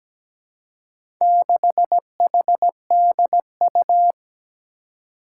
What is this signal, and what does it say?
Morse code, heard as a keyed tone, about 17 wpm, 700 Hz. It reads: 6HDU
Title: Four Challenging Letter-Number Combinations 17wpm